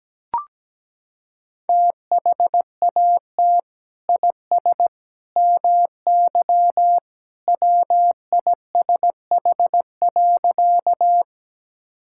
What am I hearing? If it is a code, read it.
Morse code, heard as a keyed tone, about 17 wpm, 700 Hz. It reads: THAT IS MY WISH.
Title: Sentences from Top 1600 Words 17wpm